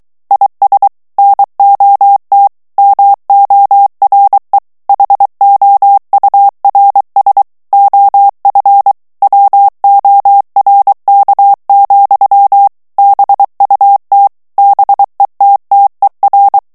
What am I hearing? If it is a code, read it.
Morse code, heard as a keyed tone, about 23 wpm, 800 Hz. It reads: IS NOT MORE HOURS OF WORK, BUT BETTER
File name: morse mp3 / ANGLAIS texte clair 1 vitesse 22 mots minute